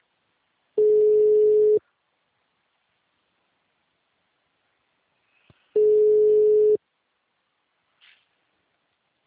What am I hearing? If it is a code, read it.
Ringback tone (Europe, 425 Hz).